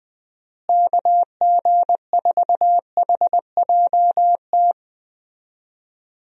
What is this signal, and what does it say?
Morse code, heard as a keyed tone, about 21 wpm, 700 Hz. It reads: KG4HJT
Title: USA Call Signs - 6 Characters - Repeated 3x - 20wpm